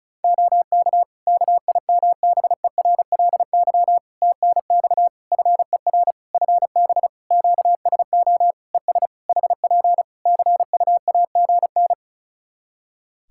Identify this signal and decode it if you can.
Morse code, heard as a keyed tone, about 36 wpm, 700 Hz. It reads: OK KIMBERLY TNX FER FB QSO ES HP CUAGN
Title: Practice QSO - Exchange 4 - Ending - 35wpm